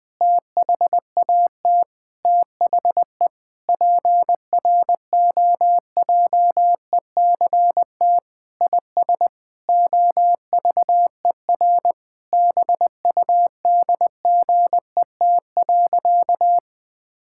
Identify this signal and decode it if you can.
Morse code, heard as a keyed tone, about 20 wpm, 700 Hz. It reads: THAT THE PROJECT IS OVER BUDGET.
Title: Sentences with 8-Word Phrases - 20wpm